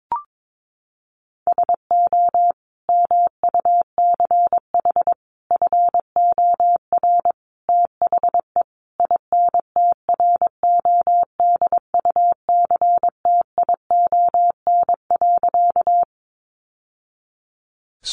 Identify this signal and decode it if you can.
Morse code, heard as a keyed tone, about 22 wpm, 700 Hz. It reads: SO MUCH FOR THE INTRODUCTION.
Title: Sentences from Top 1000 Words 22wpm